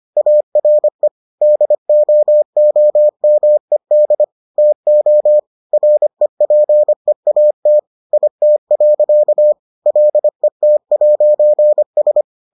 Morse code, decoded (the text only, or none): ARE DOOMED TO REPEAT IT. LET'S